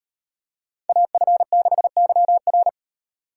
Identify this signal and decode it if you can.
Morse code, heard as a keyed tone, about 36 wpm, 700 Hz. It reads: AF6YR